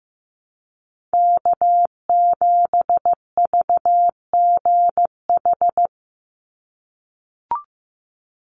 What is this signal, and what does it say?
Morse code, heard as a keyed tone, about 15 wpm, 700 Hz. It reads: K7VGH
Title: USA Call Signs 15wpm